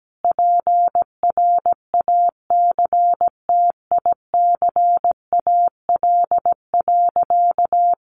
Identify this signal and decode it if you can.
Morse code, heard as a keyed tone, about 17 wpm, 700 Hz. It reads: PRACTICAL.
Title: Sentences from Top 1400 Words 17wpm